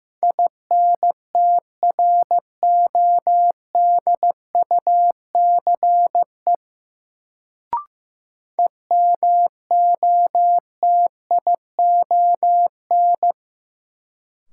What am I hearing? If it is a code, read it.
Morse code, heard as a keyed tone, about 15 wpm, 700 Hz. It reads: INTRODUCE EMOTION
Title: Sentences from Top 1600 Words - New Word Drill - 15wpm